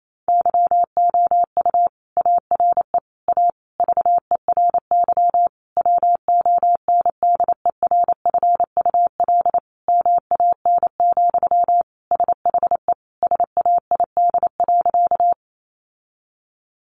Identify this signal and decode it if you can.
Morse code, heard as a keyed tone, about 28 wpm, 700 Hz. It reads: YOU ARE A VERY WONDERFUL MAN, SHE SAID.
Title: Sentences from Top 1100 Words 28wpm